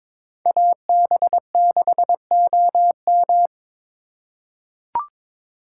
Morse code, heard as a keyed tone, about 21 wpm, 700 Hz. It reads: AB6OM
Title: USA Call Signs - 3x Character Spacing - 22wpm